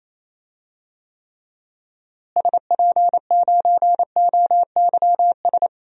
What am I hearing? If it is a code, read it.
Morse code, heard as a keyed tone, about 28 wpm, 700 Hz. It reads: SP9OYS
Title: DXCC-100 Call Signs - 6 Characters - Repeated 2x - 28wpm